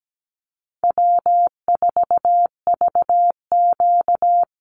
Morse code, heard as a keyed tone, about 17 wpm, 700 Hz. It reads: W4VQ
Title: Easy Call Signs - 17wpm